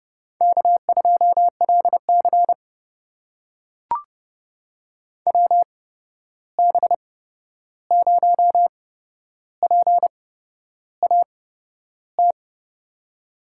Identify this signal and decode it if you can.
Morse code, heard as a keyed tone, about 30 wpm, spaced as for 11 wpm, 700 Hz. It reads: K2LC WB0PAT
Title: USA Call Signs - 8x Character Spacing - 30wpm